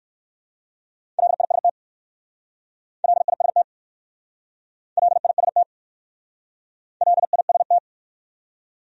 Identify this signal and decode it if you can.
Morse code, heard as a keyed tone, about 59 wpm, 700 Hz. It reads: LIST LIST LIST LIST